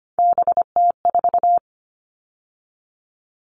Morse code, heard as a keyed tone, about 25 wpm, 700 Hz. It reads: BT4
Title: All Cut Numbers - 3-Digit 25wpm